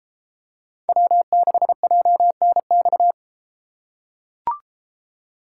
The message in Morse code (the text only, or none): W6JNX